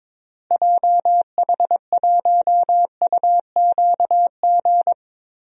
Morse code, heard as a keyed tone, about 22 wpm, 700 Hz. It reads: JH1UQG
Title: DXCC-100 Call Signs - 6 Characters - Repeated 3x - 22wpm